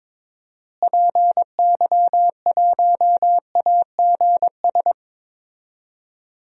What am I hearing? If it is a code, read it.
Morse code, heard as a keyed tone, about 22 wpm, 700 Hz. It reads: PY1AGS
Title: DXCC-100 Call Signs - 6 Characters - Repeated 2x - 22wpm